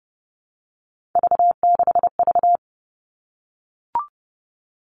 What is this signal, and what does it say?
Morse code, heard as a keyed tone, about 30 wpm, 700 Hz. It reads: V6V
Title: All Cut Numbers - 3-Digit 30wpm